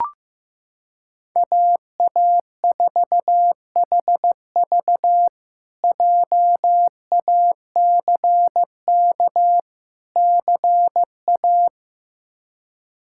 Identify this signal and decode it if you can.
Morse code, heard as a keyed tone, about 15 wpm, 700 Hz. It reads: AA4HV JACK CA
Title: K1USN SST QSO Practice - 15wpm